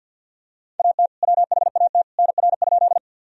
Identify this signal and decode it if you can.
Morse code, heard as a keyed tone, about 52 wpm, 700 Hz. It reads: AT WHAT NR?
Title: Sentences from Top 100 Words and Abbreviations 50wpm